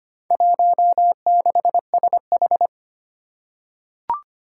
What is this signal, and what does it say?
Morse code, heard as a keyed tone, about 25 wpm, 700 Hz. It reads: 16SH
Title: Four Challenging Letter-Number Combinations 25wpm